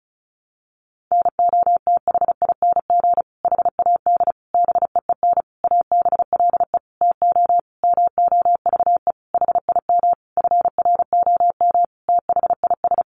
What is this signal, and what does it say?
Morse code, heard as a keyed tone, about 34 wpm, 700 Hz. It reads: NOTHING HAD BEEN ABLE TO MOVE HIM FROM THIS
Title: Sentences from Top 1300 Words 35wpm